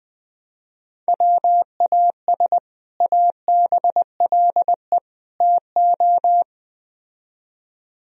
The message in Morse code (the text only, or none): WAS ABLE TO